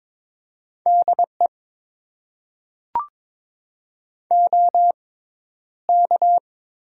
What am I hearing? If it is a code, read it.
Morse code, heard as a keyed tone, about 22 wpm, spaced as for 11 wpm, 700 Hz. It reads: DE OK
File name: Top 90 QSO Elements - 6x Character Spacing - 22wpm